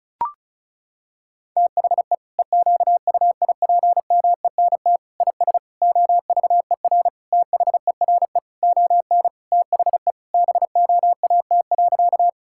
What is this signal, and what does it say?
Morse code, heard as a keyed tone, about 36 wpm, 700 Hz. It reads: THE EQUIPMENT IS OVER THERE ON THE BOAT.
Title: Sentences from Top 1000 Words - Review Part 5 - 35wpm